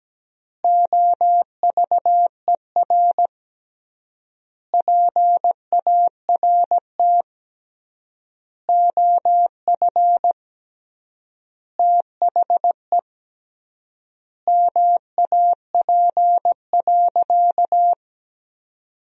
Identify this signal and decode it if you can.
Morse code, heard as a keyed tone, about 17 wpm, 700 Hz. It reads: OVER PART OF THE MAP.
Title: 3x Word-Spacing - Sentences from Top 400 Words - 17wpm